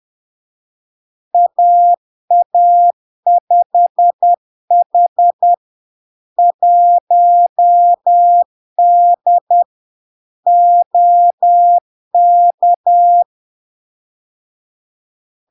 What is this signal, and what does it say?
Morse code, heard as a keyed tone, about 10 wpm, 700 Hz. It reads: AA5H 1D OK
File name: LARK-FD-2022-Log-in-CW-10wpm-700-Hz